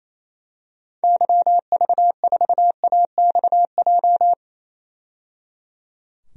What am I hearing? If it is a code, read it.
Morse code, heard as a keyed tone, about 28 wpm, 700 Hz. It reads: YV4AXJ
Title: DXCC-100 Call Signs - 6 Characters - 28wpm